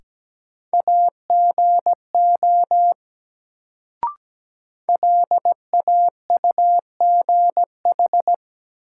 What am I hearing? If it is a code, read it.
Morse code, heard as a keyed tone, about 17 wpm, 700 Hz. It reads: AGO LAUGH
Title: Sentences from Top 500 Words - New Word Drill - 17wpm